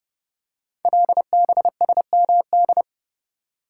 Morse code, heard as a keyed tone, about 28 wpm, 700 Hz. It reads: LBSMD
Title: Code Groups - 5 Letters - 30wpm